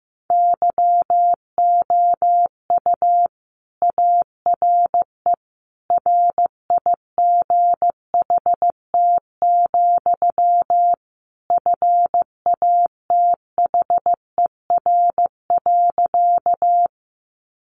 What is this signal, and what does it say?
Morse code, heard as a keyed tone, about 15 wpm, 700 Hz. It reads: YOU ARE RIGHT, FATHER.
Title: Sentences from Top 300 Words 15wpm